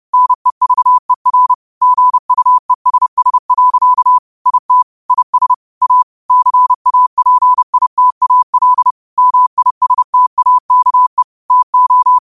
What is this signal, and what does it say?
Morse code, heard as a keyed tone, about 30 wpm, 1000 Hz. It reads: NEVER GUESS. IT IS A CAPITAL MISTAKE TO